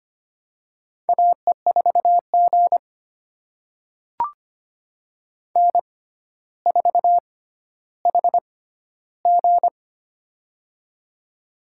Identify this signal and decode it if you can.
Morse code, heard as a keyed tone, about 25 wpm, spaced as for 13 wpm, 700 Hz. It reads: AE4G N4HG
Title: USA Call Signs - 4 Characters - 6x Character Spacing - 25wpm